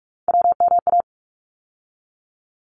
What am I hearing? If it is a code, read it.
Morse code, heard as a keyed tone, about 44 wpm, 700 Hz. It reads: WMA
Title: ARRL Sections 45wpm